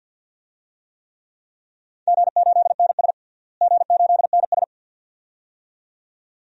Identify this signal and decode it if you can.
Morse code, heard as a keyed tone, about 51 wpm, 700 Hz. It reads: G8NS G8NS
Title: DXCC-100 Call Signs - 4 Characters - Repeated 2x - 50wpm